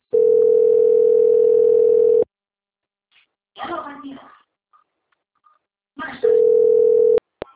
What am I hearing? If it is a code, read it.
Ringback tone (North America), 440 + 480 Hz, with speech.
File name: out- - - - - .30249